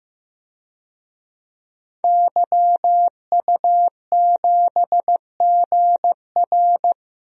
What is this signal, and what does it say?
Morse code, heard as a keyed tone, about 14 wpm, 700 Hz. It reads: YU7GR
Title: DXCC-100 Call Signs - 5 Characters - Repeated 2x - 15wpm